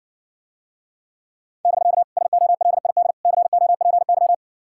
Morse code, heard as a keyed tone, about 60 wpm, 700 Hz. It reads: <BK> SOLID COPY